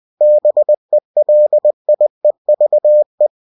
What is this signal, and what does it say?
Morse code, heard as a keyed tone, about 20 wpm, 600 Hz. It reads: BELIEVE
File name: cw-books / winnie-the-pooh / winnie-20wpm-page-0027